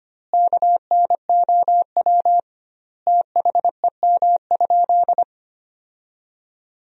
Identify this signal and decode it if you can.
Morse code, heard as a keyed tone, about 25 wpm, 700 Hz. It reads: KNOW THEM?
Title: 2x Word-Spacing - Sentences from Top 100 Words - 25wpm